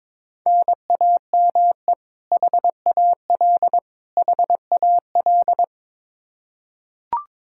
Morse code, heard as a keyed tone, about 22 wpm, 700 Hz. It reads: NAME HAL HAL
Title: Practice QSO - Exchange 1 - Beginning - 22wpm